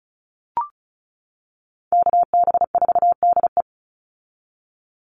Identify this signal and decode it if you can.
Morse code, heard as a keyed tone, about 34 wpm, 700 Hz. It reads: KB4DE
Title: Call Signs - 35wpm